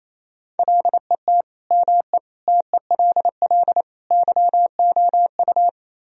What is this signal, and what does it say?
Morse code, heard as a keyed tone, about 35 wpm, 700 Hz. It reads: LET ME TELL YOU